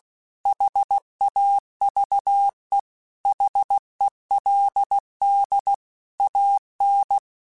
Morse code, heard as a keyed tone, about 16 wpm, 800 Hz. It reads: HAVE HELD AN
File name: QSOs / QSOs-mp3s / WA7QOX 2@15